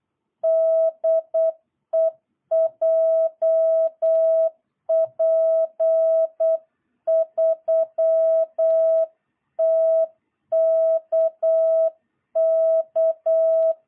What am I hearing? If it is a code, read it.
Morse code, heard as a keyed tone, about 8 wpm, 660 Hz. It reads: DE JP3TKK
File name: jh8cbh / 2022cw